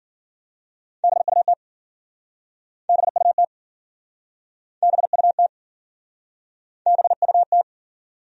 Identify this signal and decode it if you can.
Morse code, heard as a keyed tone, about 53 wpm, 700 Hz. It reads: BUT BUT BUT BUT